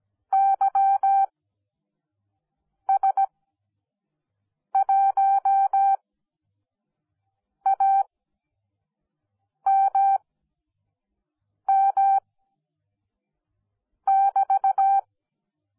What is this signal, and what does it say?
Morse code, heard as a keyed tone, about 17 wpm, spaced as for 5 wpm, 780 Hz. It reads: YS1AMM<BT>